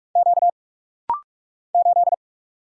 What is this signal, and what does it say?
Morse code, heard as a keyed tone, about 44 wpm, 700 Hz. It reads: Q 8
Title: Single Letter-Number Alt - Rapid-Fire 45wpm